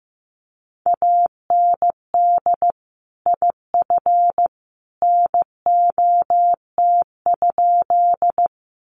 Morse code, heard as a keyed tone, about 15 wpm, 700 Hz. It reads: AND IF NOT?
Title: Sentences from Top 600 Words 15wpm